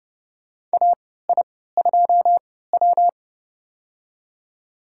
Morse code, heard as a keyed tone, about 30 wpm, spaced as for 17 wpm, 700 Hz. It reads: AI2W
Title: USA Call Signs - 4 Characters - 3x Character Spacing - 30wpm